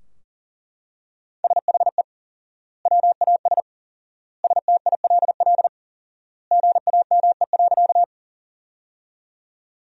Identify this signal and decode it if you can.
Morse code, heard as a keyed tone, about 40 wpm, 700 Hz. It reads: SHE WAS STILL GAME.
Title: 4x Word-Spacing - Sentences from Top 500 Words - 40wpm